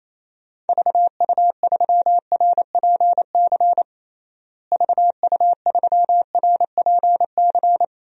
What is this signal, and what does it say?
Morse code, heard as a keyed tone, about 27 wpm, 700 Hz. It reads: VU3RPC VU3RPC
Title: DXCC-100 Call Signs - 6 Characters - Repeated 2x - 28wpm